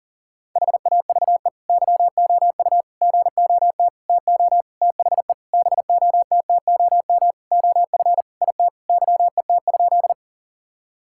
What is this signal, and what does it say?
Morse code, heard as a keyed tone, about 40 wpm, 700 Hz. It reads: HAVE YOU GOT TO THE BOTTOM OF IT YET?
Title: Sentences from Top 1000 Words - Review Part 6 - 40wpm